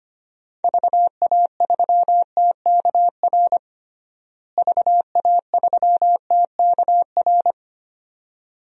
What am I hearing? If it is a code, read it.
Morse code, heard as a keyed tone, about 25 wpm, 700 Hz. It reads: VA3TKR VA3TKR
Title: DXCC-100 Call Signs - 6 Characters - Repeated 2x - 25wpm